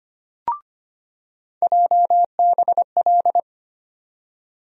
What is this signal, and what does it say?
Morse code, heard as a keyed tone, about 25 wpm, 700 Hz. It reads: JBL
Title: USA Call Sign Postfixes - 3 Letters 25wpm